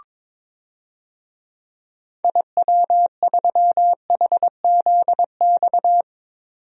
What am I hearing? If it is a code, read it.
Morse code, heard as a keyed tone, about 22 wpm, 700 Hz. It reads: IW3HZX
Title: DXCC-100 Call Signs - 6 Characters - 22wpm